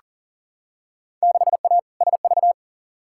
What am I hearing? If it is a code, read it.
Morse code, heard as a keyed tone, about 40 wpm, 700 Hz. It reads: 6A SV
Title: Field Day Exchanges Encore - 40wpm